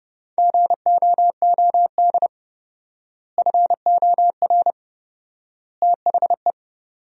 Morse code, heard as a keyed tone, about 30 wpm, 700 Hz. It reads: GOOD FOR THE
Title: 4x Word-Spacing - Sentences from Top 300 Words - 30wpm